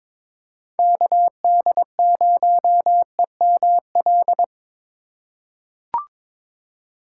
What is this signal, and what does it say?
Morse code, heard as a keyed tone, about 22 wpm, 700 Hz. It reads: KD0EML
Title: USA Call Signs - 3x Character Spacing - 22wpm